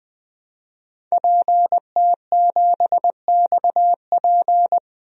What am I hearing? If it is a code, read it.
Morse code, heard as a keyed tone, about 20 wpm, 700 Hz. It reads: PT7XP
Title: DXCC-100 Call Signs - 5 Characters - 20wpm